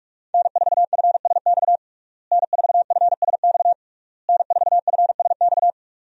Morse code, heard as a keyed tone, about 48 wpm, 700 Hz. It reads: N4FSX N4FSX N4FSX
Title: USA Call Signs - 5 Characters - Repeated 3x - 45wpm